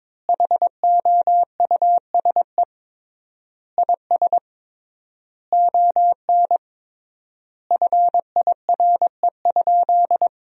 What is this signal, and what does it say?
Morse code, heard as a keyed tone, about 22 wpm, 700 Hz. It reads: HOUSE IS ON FIRE?
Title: 3x Word-Spacing - Sentences from Top 400 Words - 22wpm